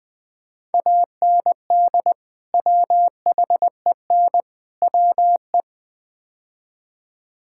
Morse code, heard as a keyed tone, about 21 wpm, 700 Hz. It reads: AND WHEN WE
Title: Build A Phrase - 5-Word Phrases - 20wpm